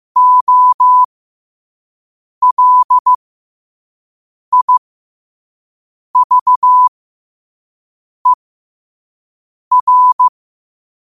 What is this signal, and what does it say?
Morse code, heard as a keyed tone, about 15 wpm, spaced as for 5 wpm, 1000 Hz. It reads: OLIVER
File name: qotdfiles / 5wpm